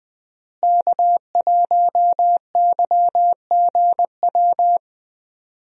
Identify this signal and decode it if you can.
Morse code, heard as a keyed tone, about 20 wpm, 700 Hz. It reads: K1YGW
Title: USA Call Signs - 5 Characters - Repeated 3x - 20wpm